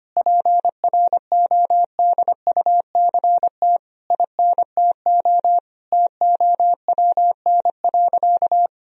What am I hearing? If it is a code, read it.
Morse code, heard as a keyed tone, about 25 wpm, 700 Hz. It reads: PRODUCT INTO TOWN.
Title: Sentences from Top 1000 Words - Review Part 2 - 25wpm